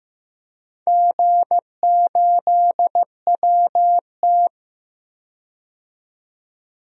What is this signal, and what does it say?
Morse code, heard as a keyed tone, about 15 wpm, 700 Hz. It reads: G8WT
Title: DXCC-100 Call Signs - 4 Characters - 15wpm